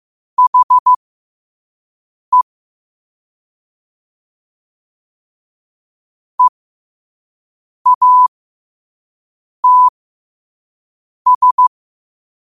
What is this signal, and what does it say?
Morse code, heard as a keyed tone, about 15 wpm, spaced as for 5 wpm, 1000 Hz. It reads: HE EATS
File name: qotdfiles / 5wpm